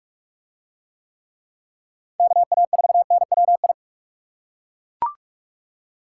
Morse code, heard as a keyed tone, about 40 wpm, 700 Hz. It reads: KA4NWI